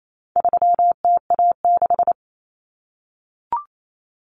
Morse code, heard as a keyed tone, about 28 wpm, 700 Hz. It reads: 3TA6
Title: Common Cut Numbers - 4-Digit 28wpm